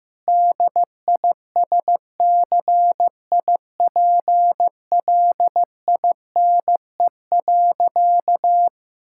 Morse code, heard as a keyed tone, about 15 wpm, 700 Hz. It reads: DISCIPLINE.
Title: Sentences from Top 1000 Words - Review Part 6 - 15wpm